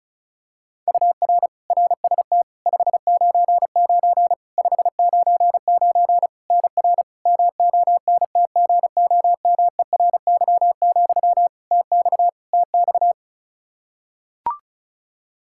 Morse code, heard as a keyed tone, about 34 wpm, 700 Hz. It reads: UR RST 599 599 NR MONTGOMERY, TX TX
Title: Practice QSO - Exchange 1 - Beginning - 35wpm